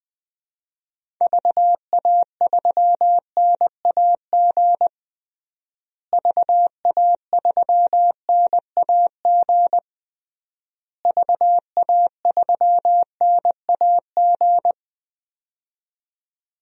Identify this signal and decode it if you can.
Morse code, heard as a keyed tone, about 20 wpm, 700 Hz. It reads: VA3NAG VA3NAG VA3NAG
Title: DXCC-100 Call Signs - 6 Characters - Repeated 3x - 20wpm